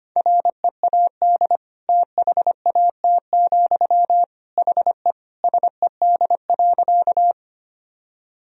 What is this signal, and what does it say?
Morse code, heard as a keyed tone, about 25 wpm, 700 Hz. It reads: READ THAT, HE SED.
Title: Sentences from Top 200 Words and Abbreviations 25wpm